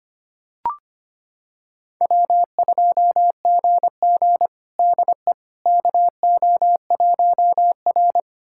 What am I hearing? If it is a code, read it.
Morse code, heard as a keyed tone, about 25 wpm, 700 Hz. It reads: W2GG DE KO1R
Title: USA Callsign DE Callsign - 4 Characters - 25wpm